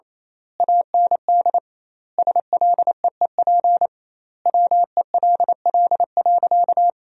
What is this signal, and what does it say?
Morse code, heard as a keyed tone, about 28 wpm, 700 Hz. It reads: AND SLEEP WELL.
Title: 2x Word-Spacing - Sentences from Top 500 Words - 28wpm